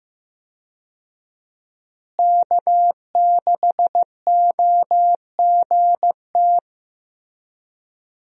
Morse code, heard as a keyed tone, about 15 wpm, 700 Hz. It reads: K6OGT
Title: USA Call Signs - 5 Characters - Repeated 3x - 15wpm